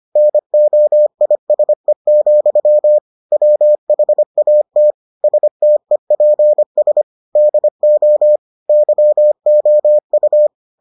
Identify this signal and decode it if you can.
Morse code, heard as a keyed tone, about 25 wpm, 600 Hz. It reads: NOISE, WHAT STEPS DO YOU